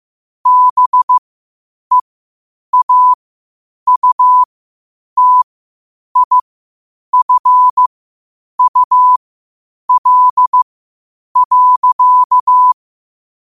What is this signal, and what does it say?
Morse code, heard as a keyed tone, about 15 wpm, spaced as for 8 wpm, 1000 Hz. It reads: BEAUTIFUL.
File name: qotdfiles / 8wpm